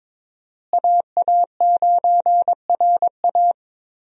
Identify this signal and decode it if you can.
Morse code, heard as a keyed tone, about 22 wpm, 700 Hz. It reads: AA9RA
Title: USA Call Signs - 5 Characters - Repeated 2x - 22wpm